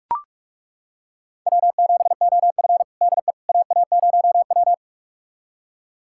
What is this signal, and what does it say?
Morse code, heard as a keyed tone, about 41 wpm, 700 Hz. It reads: W7OF DE AA0W
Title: USA Callsign DE Callsign - 4 Characters - 45wpm